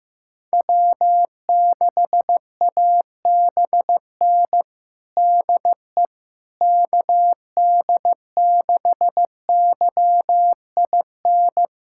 Morse code, heard as a keyed tone, about 15 wpm, 700 Hz. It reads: W6ABN DE KD6YIN
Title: USA Callsign DE Callsign 15wpm